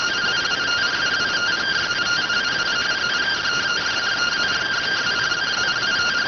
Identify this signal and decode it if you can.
Two-tone teleprinter signal, about 170 Hz shift.